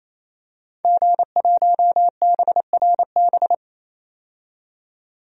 Morse code, heard as a keyed tone, about 27 wpm, 700 Hz. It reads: G1BRB
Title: DXCC-100 Call Signs - 5 Characters - 28wpm